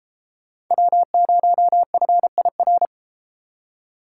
Morse code, heard as a keyed tone, about 30 wpm, 700 Hz. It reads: W0FIR